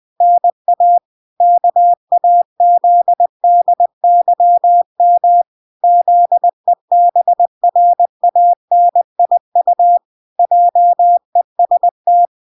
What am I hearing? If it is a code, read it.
Morse code, heard as a keyed tone, about 20 wpm, 700 Hz. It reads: NA KAZDYM ZEBRANIU JEST